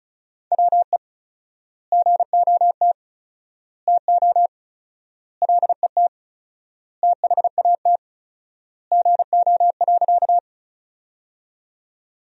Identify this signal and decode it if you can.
Morse code, heard as a keyed tone, about 34 wpm, 700 Hz. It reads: WE GOT TO LET THAT GO.
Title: 4x Word-Spacing - Sentences from Top 300 Words - 35wpm